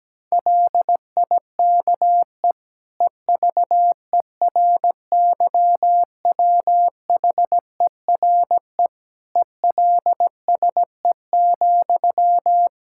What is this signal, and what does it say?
Morse code, heard as a keyed tone, about 17 wpm, 700 Hz. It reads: LIKE EVERYWHERE ELSE,
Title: Sentences from Top 1600 Words 17wpm